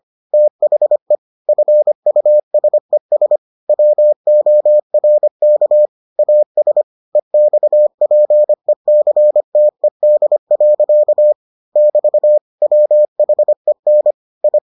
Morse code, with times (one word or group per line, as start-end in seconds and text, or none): 0.3-1.1 THE
1.5-3.4 FUSES
3.7-5.9 WORK
6.2-6.8 AS
7.2-11.3 EXPECTED.
11.8-12.4 <BT>
12.6-14.1 WHEN
14.4-14.6 I